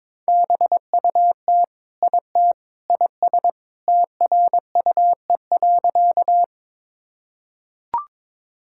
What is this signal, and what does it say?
Morse code, heard as a keyed tone, about 22 wpm, 700 Hz. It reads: BUT IT IS TRUE.
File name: practice / 22wpm / Sentences with 3-Word Phrases - 22wpm